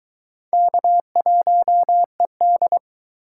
Morse code, heard as a keyed tone, about 20 wpm, 700 Hz. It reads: K1ED